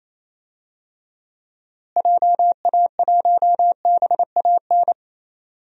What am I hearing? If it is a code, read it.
Morse code, heard as a keyed tone, about 27 wpm, 700 Hz. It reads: JA1BAN
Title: DXCC-100 Call Signs - 6 Characters - Repeated 2x - 28wpm